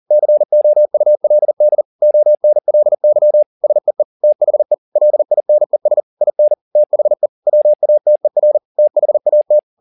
Morse code, heard as a keyed tone, about 40 wpm, 600 Hz. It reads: COULD ONLY SEE THE LINES IN THE WATER THAT
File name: the old man and the sea 2 40wpm